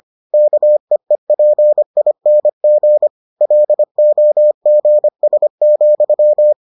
Morse code, heard as a keyed tone, about 25 wpm, 600 Hz. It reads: KEEPING LOGS,